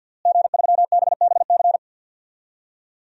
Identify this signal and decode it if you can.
Morse code, heard as a keyed tone, about 52 wpm, 700 Hz. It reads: G3BBC
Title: DXCC-100 Call Signs - 5 Characters - Repeated 3x - 50wpm